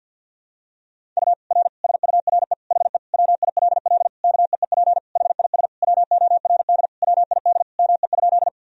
Morse code, heard as a keyed tone, about 52 wpm, 700 Hz. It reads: U R SURE HE WILL KEEP HIS WORD WID ME?
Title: Sentences from Top 300 Words and Abbreviations 50wpm